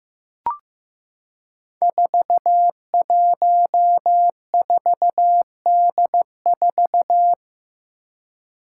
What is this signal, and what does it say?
Morse code, heard as a keyed tone, about 15 wpm, 700 Hz. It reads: 414D4
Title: Five Challenging Letter-Number Combinations 15wpm